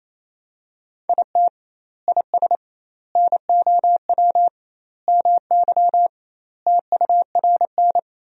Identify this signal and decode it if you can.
Morse code, heard as a keyed tone, about 28 wpm, 700 Hz. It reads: IT IS NOW MY TURN
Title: 2x Word-Spacing - Sentences from Top 200 Words - 28wpm